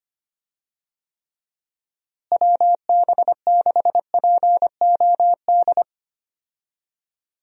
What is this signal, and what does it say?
Morse code, heard as a keyed tone, about 25 wpm, 700 Hz. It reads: WB6POD
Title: USA Call Signs - 6 Characters - Repeated 3x - 25wpm